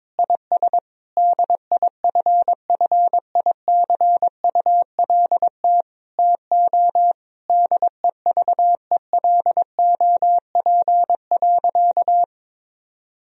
Morse code, heard as a keyed tone, about 22 wpm, 700 Hz. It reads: IS DIFFICULT TO DEVELOP.
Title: Sentences from Top 1000 Words - Review Part 2 - 22wpm